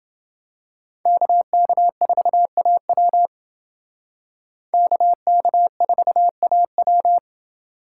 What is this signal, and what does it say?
Morse code, heard as a keyed tone, about 29 wpm, 700 Hz. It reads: KK4AW KK4AW